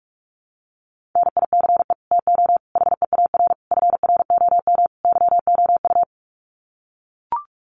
Morse code, heard as a keyed tone, about 44 wpm, 700 Hz. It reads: NICE TO HEAR FROM YOU
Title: Common QSO Phrases 45wpm